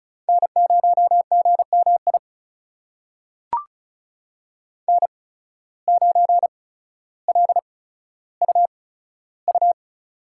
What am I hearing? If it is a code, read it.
Morse code, heard as a keyed tone, about 34 wpm, spaced as for 15 wpm, 700 Hz. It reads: N0GMI N9LUU
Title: USA Call Signs - 5 Characters - 8x Character Spacing - 35wpm